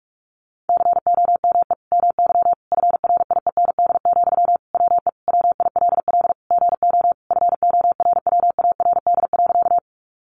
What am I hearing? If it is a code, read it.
Morse code, heard as a keyed tone, about 45 wpm, 700 Hz. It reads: COME MY FRIEND, WE WILL GO FORWARD.
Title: Sentences from Top 1300 Words 45wpm